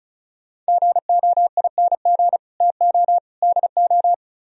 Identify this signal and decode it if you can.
Morse code, heard as a keyed tone, about 45 wpm, 700 Hz. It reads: GOING TO DO